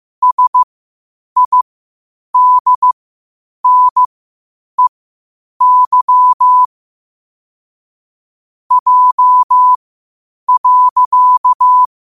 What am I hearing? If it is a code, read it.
Morse code, heard as a keyed tone, about 15 wpm, spaced as for 8 wpm, 1000 Hz. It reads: SIDNEY J.
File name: qotdfiles / 8wpm